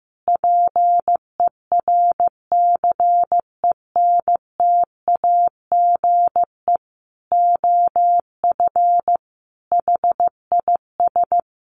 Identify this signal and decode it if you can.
Morse code, heard as a keyed tone, about 15 wpm, 700 Hz. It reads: PERCENTAGE OF HIS
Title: Sentences from Top 1300 Words 15wpm